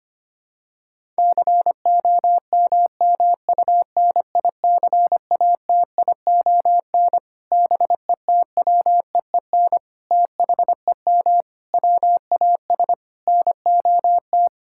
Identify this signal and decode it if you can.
Morse code, heard as a keyed tone, about 25 wpm, 700 Hz. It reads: COMMUNICATION BETWEEN THEM WAS NOT
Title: Sentences from Top 1000 Words - Review Part 6 - 25wpm